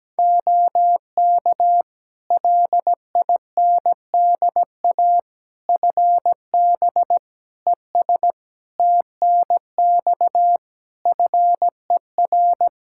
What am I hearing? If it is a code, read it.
Morse code, heard as a keyed tone, about 17 wpm, 700 Hz. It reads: OK LINDA FB ES TNX FER
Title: Practice QSO - Exchange 2 - Info - 17wpm